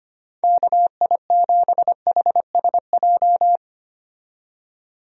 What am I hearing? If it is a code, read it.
Morse code, heard as a keyed tone, about 25 wpm, 700 Hz. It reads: KI7HSJ
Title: USA Call Signs - 6 Characters - 25wpm